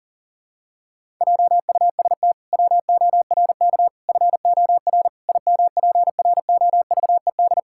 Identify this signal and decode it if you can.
Morse code, heard as a keyed tone, about 40 wpm, 700 Hz. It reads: JUST WORK FOR IMPROVED
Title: Pangram Challenge v2 - 40wpm